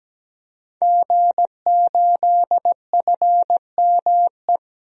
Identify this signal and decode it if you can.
Morse code, heard as a keyed tone, about 17 wpm, 700 Hz. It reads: G8FME
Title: DXCC-100 Call Signs - 5 Characters - 17wpm